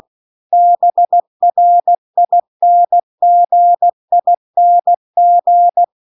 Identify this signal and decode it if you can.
Morse code, heard as a keyed tone, about 16 wpm, 700 Hz. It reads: BRINGING